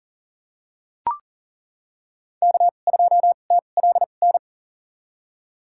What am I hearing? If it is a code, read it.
Morse code, heard as a keyed tone, about 40 wpm, spaced as for 29 wpm, 700 Hz. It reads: K2TLN
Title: USA Call Signs - 5 Characters - 2x Character Spacing - 40wpm